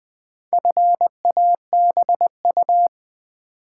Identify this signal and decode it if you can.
Morse code, heard as a keyed tone, about 20 wpm, 700 Hz. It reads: FABU
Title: Build A Word - 2 Letters At A Time - 8-Letter Words - 20wpm